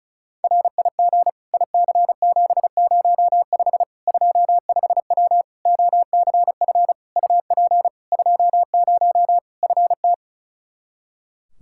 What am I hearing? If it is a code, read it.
Morse code, heard as a keyed tone, about 34 wpm, 700 Hz. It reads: RIG IC705 25W OCF UP 20 FT
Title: K1USN SST QSO Practice 35wpm